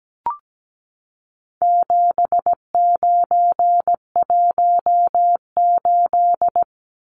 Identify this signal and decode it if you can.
Morse code, heard as a keyed tone, about 17 wpm, 700 Hz. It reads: 7918
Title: Numbers - 4-Digit 17wpm